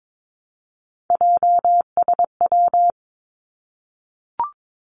Morse code, heard as a keyed tone, about 22 wpm, 700 Hz. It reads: JSW
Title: Three Challenging Letter-Number Combinations 22wpm